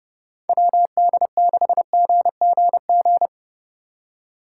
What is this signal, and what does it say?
Morse code, heard as a keyed tone, about 30 wpm, 700 Hz. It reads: WD6GGG
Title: USA Call Signs - 6 Characters - 3x Character Spacing - 30wpm